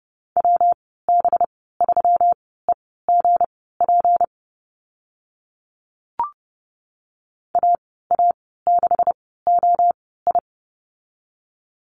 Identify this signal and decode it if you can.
Morse code, heard as a keyed tone, about 30 wpm, spaced as for 17 wpm, 700 Hz. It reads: WB3EGP AA6OI
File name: Lesson 37g - Letter x - Call Signs 30wpm